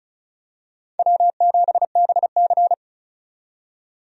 Morse code, heard as a keyed tone, about 34 wpm, 700 Hz. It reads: W7BC
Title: USA Call Signs - 4 Characters - 35wpm